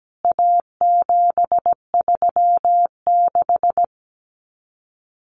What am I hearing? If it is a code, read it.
Morse code, heard as a keyed tone, about 17 wpm, 700 Hz. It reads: A736
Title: CWT Exchange 17wpm